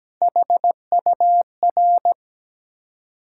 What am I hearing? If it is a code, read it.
Morse code, heard as a keyed tone, about 17 wpm, 700 Hz. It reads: HUR
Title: Ontario QSO Party Exchange 17wpm